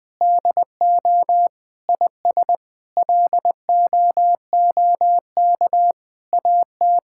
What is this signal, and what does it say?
Morse code, heard as a keyed tone, about 21 wpm, 700 Hz. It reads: DO IS LOOK AT
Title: Sets of 9 Words - 20wpm